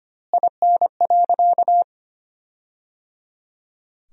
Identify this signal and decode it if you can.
Morse code, heard as a keyed tone, about 25 wpm, 700 Hz. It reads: IN.
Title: 2x Word-Spacing - Sentences from Top 300 Words - 25wpm